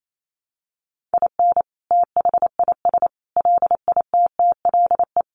Morse code, heard as a keyed tone, about 28 wpm, 700 Hz. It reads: IN THIS LITTLE
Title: Sentences from Top 1200 Words 28wpm